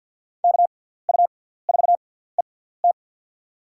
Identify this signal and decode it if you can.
Morse code, heard as a keyed tone, about 52 wpm, spaced as for 17 wpm, 700 Hz. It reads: KU4ET
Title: USA Call Signs - 5 Characters - 6x Character Spacing - 50wpm